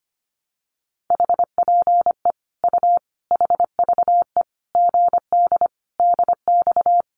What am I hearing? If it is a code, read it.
Morse code, heard as a keyed tone, about 25 wpm, 700 Hz. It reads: HPE U HVE GD DX
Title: Common QSO Phrases 25wpm